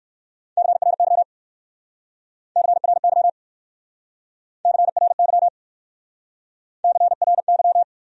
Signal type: Morse code

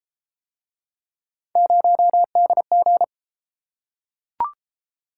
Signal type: Morse code

